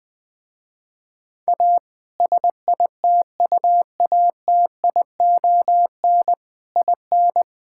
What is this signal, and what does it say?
Morse code, heard as a keyed tone, about 20 wpm, 700 Hz. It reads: A SITUATION IN
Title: Sentences with 3-Word Phrases - Spoken Beginning - 20wpm